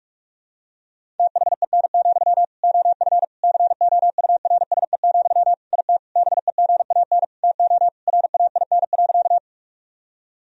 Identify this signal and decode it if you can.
Morse code, heard as a keyed tone, about 41 wpm, 700 Hz. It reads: THEN, OF COURSE, IT BEGAN TO RAIN.
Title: Sentences from Top 1000 Words - Review Part 1 - 45wpm